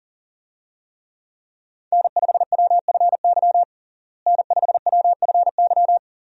Morse code, heard as a keyed tone, about 40 wpm, 700 Hz. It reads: N5WFY N5WFY
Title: USA Call Signs - 5 Characters - Repeated 3x - 40wpm